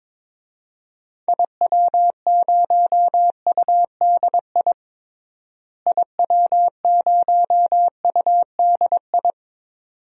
Morse code, heard as a keyed tone, about 22 wpm, 700 Hz. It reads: IW0UDI IW0UDI
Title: DXCC-100 Call Signs - 6 Characters - Repeated 2x - 22wpm